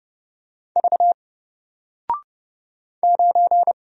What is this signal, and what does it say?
Morse code, heard as a keyed tone, about 30 wpm, 700 Hz. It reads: V 9
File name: Single-Letter-Number-30wpm